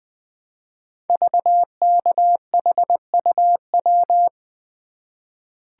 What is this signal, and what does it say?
Morse code, heard as a keyed tone, about 20 wpm, 700 Hz. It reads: VKHUW
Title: Code Groups - 5 Letters - 20wpm